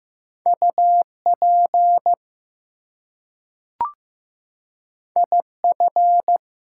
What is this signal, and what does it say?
Morse code, heard as a keyed tone, about 15 wpm, 700 Hz. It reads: UP IF
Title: Two Letter Words - 15wpm